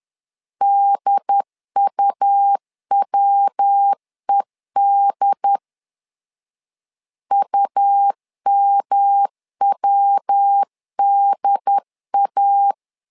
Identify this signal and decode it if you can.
Morse code, heard as a keyed tone, about 10 wpm, 800 Hz. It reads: DUWED UMWDA
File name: LEZIONI / 3 MWRO / 3 MW 10wpm 1959